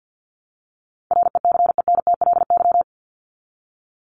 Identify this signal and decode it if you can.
Morse code, heard as a keyed tone, about 50 wpm, 700 Hz. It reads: RECENTLY
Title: LY Suffix - Top 100 Words with Base Words 50wpm